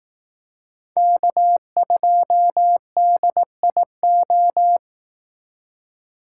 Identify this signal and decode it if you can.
Morse code, heard as a keyed tone, about 18 wpm, 700 Hz. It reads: K2DIO
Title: Speed Racing - USA 1x3 Call Signs 15wpm